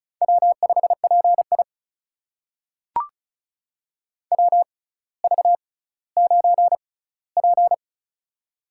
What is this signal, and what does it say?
Morse code, heard as a keyed tone, about 36 wpm, spaced as for 18 wpm, 700 Hz. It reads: W5PI WV9P
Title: USA Call Signs - 4 Characters - 6x Character Spacing - 35wpm